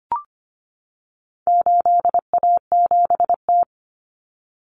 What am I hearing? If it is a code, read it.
Morse code, heard as a keyed tone, about 25 wpm, 700 Hz. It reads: 8A7T
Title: Common Cut Numbers - 4-Digit 25wpm